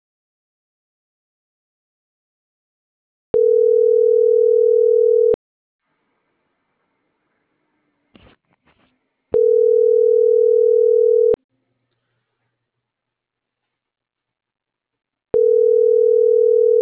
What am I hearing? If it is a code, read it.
Ringback tone (North America), 440 + 480 Hz.